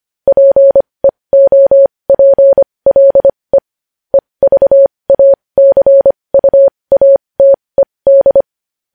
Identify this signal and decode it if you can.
Morse code, heard as a keyed tone, about 25 wpm, spaced as for 20 wpm, 550 Hz. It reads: PEOPLE EVACUATED
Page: News Headlines in Morse Code at 20 WPM